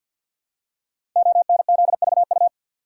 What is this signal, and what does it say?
Morse code, heard as a keyed tone, about 52 wpm, 700 Hz. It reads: ONZVU
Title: Code Groups - 5 Letter:Numbers - 50wpm